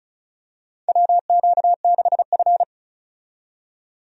Morse code, heard as a keyed tone, about 36 wpm, 700 Hz. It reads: WQ6F
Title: USA Call Signs - 4 Characters - 35wpm